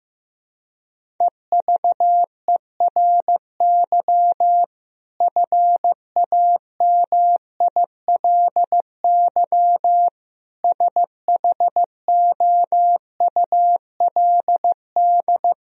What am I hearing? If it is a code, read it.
Morse code, heard as a keyed tone, about 15 wpm, 700 Hz. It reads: EVERY FAMILY SHOULD
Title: Sentences from Top 1000 Words - Review Part 1 - 15wpm